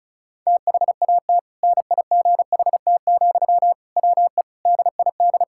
Morse code, heard as a keyed tone, about 36 wpm, 700 Hz. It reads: THAT NIGHT, WE DID
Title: Sentences from Top 1000 Words - Review Part 6 - 35wpm